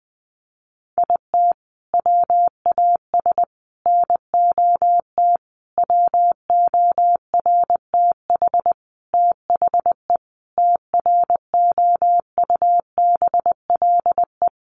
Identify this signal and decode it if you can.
Morse code, heard as a keyed tone, about 20 wpm, 700 Hz. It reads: IT WAS NOT WORTH THE TROUBLE
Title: Sentences from Top 1200 Words 20wpm